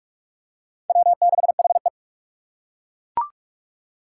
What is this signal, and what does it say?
Morse code, heard as a keyed tone, about 40 wpm, 700 Hz. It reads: W6HE